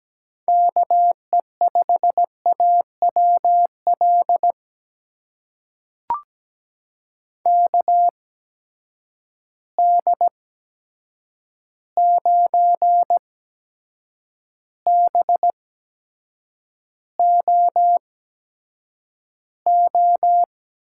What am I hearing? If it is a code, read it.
Morse code, heard as a keyed tone, about 17 wpm, spaced as for 7 wpm, 700 Hz. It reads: KE5AWL KD9BOO
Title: USA Call Signs - 6 Characters - 8x Character Spacing - 17wpm